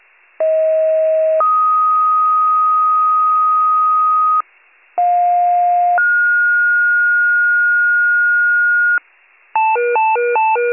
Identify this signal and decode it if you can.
Two-tone page, heard as a one-second tone, then a three-second tone, about 640 then 1250 Hz, about 710 then 1430 Hz, with noise. It is heard from 0.4 to 9.0 s.